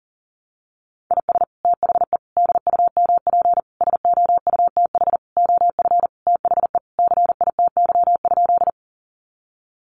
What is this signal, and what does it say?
Morse code, heard as a keyed tone, about 40 wpm, 700 Hz. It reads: IS THE DUMP SOUTH OF THE CITY?
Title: Sentences from Top 1300 Words 40wpm